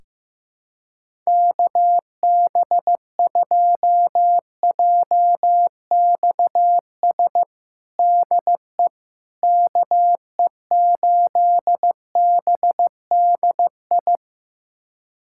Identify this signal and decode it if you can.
Morse code, heard as a keyed tone, about 15 wpm, 700 Hz. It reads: KB2JXS DE KE8BDI
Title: USA Callsign DE Callsign 15wpm